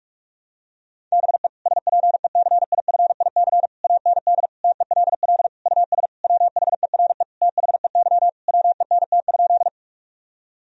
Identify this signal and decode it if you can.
Morse code, heard as a keyed tone, about 41 wpm, 700 Hz. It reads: BE SPECIFIC AND TELL US WHERE THEY WENT?
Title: Sentences from Top 1000 Words - Review Part 3 - 45wpm